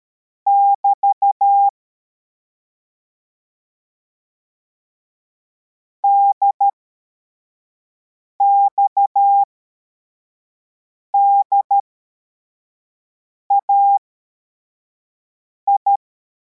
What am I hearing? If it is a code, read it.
Morse code, heard as a keyed tone, about 13 wpm, spaced as for 4 wpm, 800 Hz. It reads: <BT> DXDAI